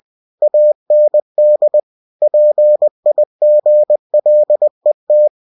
Morse code, heard as a keyed tone, about 20 wpm, 600 Hz. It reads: AND PIGLET
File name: cw-books / winnie-the-pooh / winnie-20wpm-page-0023